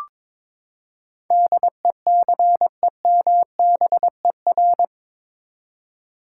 Morse code, heard as a keyed tone, about 21 wpm, 700 Hz. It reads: DECEMBER
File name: practice / 22wpm / Eight Letter Words - 22wpm